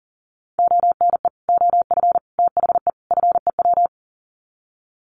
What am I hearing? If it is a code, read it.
Morse code, heard as a keyed tone, about 40 wpm, 700 Hz. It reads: ONE OF THE FEW
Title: Sets of 4 Words 40wpm